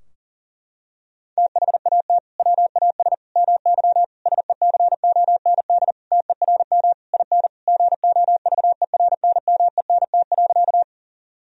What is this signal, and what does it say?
Morse code, heard as a keyed tone, about 40 wpm, 700 Hz. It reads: THAT WAS MY SECOND TERM IN GOVERNMENT.
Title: Sentences from Top 1000 Words - Review Part 4 - 40wpm